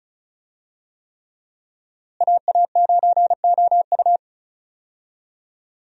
Morse code, heard as a keyed tone, about 36 wpm, 700 Hz. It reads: AA9OU
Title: USA Call Signs - 5 Characters - Repeated 3x - 35wpm